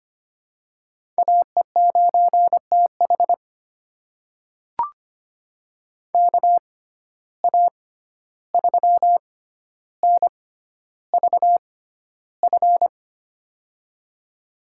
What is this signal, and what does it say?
Morse code, heard as a keyed tone, about 25 wpm, spaced as for 12 wpm, 700 Hz. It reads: AE9TH KA3NVF
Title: USA Call Signs - 6x Character Spacing - 25wpm